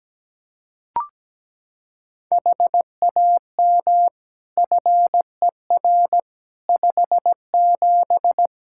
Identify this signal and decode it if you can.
Morse code, heard as a keyed tone, about 17 wpm, 700 Hz. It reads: HAM FER 57
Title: K1USN SST QSO Practice 17wpm